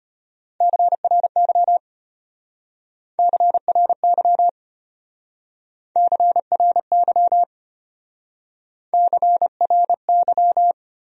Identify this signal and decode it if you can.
Morse code, heard as a keyed tone, about 33 wpm, 700 Hz. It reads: CRY CRY CRY CRY